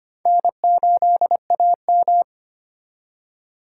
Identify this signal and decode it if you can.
Morse code, heard as a keyed tone, about 33 wpm, 700 Hz. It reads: N8AM